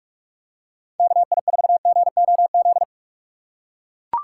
Morse code, heard as a keyed tone, about 44 wpm, 700 Hz. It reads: KI4GOZ